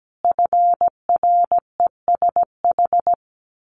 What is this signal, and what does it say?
Morse code, heard as a keyed tone, about 17 wpm, 700 Hz. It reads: FRESH
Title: A Sound Of Thunder 17wpm